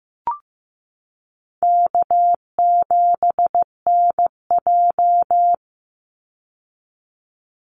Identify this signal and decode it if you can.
Morse code, heard as a keyed tone, about 15 wpm, 700 Hz. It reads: K7NJ
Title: Easy Call Signs Repeated 2x - 15wpm